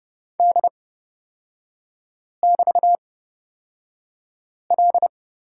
Morse code, heard as a keyed tone, about 30 wpm, spaced as for 5 wpm, 700 Hz. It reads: D<BT>L
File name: Lesson 36e - BT - Three Characters 30wpm